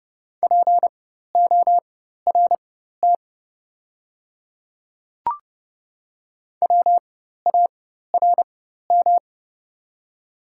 Morse code, heard as a keyed tone, about 30 wpm, spaced as for 14 wpm, 700 Hz. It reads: PORT WARM
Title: 4x Character Spacing - Top 500 Words - 30wpm